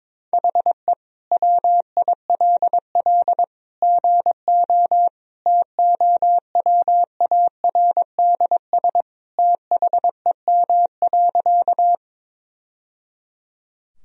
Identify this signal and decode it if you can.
Morse code, heard as a keyed tone, about 22 wpm, 700 Hz. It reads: HE WILL GO TOWARDS THEM.
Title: Sentences from Top 1000 Words - Review Part 2 - 22wpm